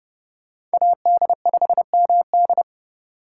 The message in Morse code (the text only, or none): AD5MD